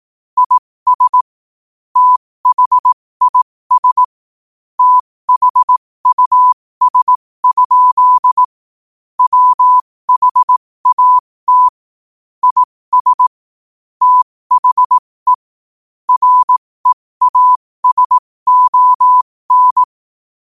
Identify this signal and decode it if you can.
Morse code, heard as a keyed tone, about 18 wpm, spaced as for 15 wpm, 1000 Hz. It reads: IS THIS THUS? WHAT IS THE REASON
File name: qotdfiles / 15wpm